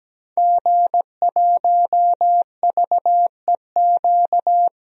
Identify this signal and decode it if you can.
Morse code, heard as a keyed tone, about 17 wpm, 700 Hz. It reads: G1VEQ
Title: DXCC-100 Call Signs - 5 Characters - 17wpm